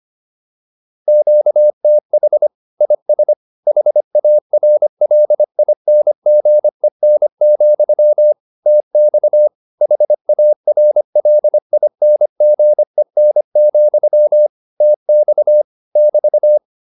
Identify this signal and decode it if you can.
Morse code, heard as a keyed tone, about 25 wpm, 600 Hz. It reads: QTH IS HARLINGEN, TX HARLINGEN, TX <BT>
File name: QSO304a 25